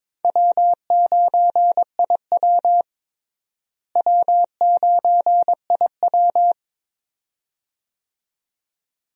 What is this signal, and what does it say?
Morse code, heard as a keyed tone, about 22 wpm, 700 Hz. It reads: W9IW W9IW
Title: USA Call Signs - 4 Characters - Repeated 2x - 22wpm